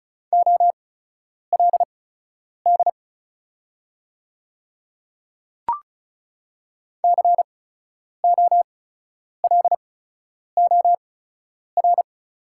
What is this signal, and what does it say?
Morse code, heard as a keyed tone, about 35 wpm, spaced as for 10 wpm, 690 Hz. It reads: OLD COLOR
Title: 8x Character Spacing - Top 300 Words - 35wpm